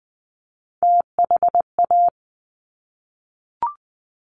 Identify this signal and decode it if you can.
Morse code, heard as a keyed tone, about 20 wpm, 700 Hz. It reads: THA
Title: Top Three Letter Combinations in Words 20wpm